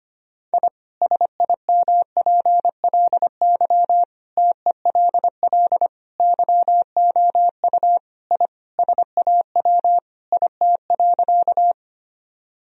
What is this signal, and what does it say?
Morse code, heard as a keyed tone, about 25 wpm, 700 Hz. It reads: I SIMPLY TELL YOU I SAW IT.
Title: Sentences from Top 1000 Words - Review Part 4 - 25wpm